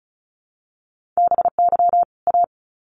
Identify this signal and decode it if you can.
Morse code, heard as a keyed tone, about 34 wpm, 700 Hz. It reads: BY A
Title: Sets of 2 Words 35wpm